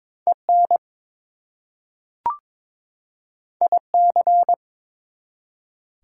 Morse code, heard as a keyed tone, about 22 wpm, 700 Hz. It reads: EN IC
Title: Prefixes and Suffixes 22wpm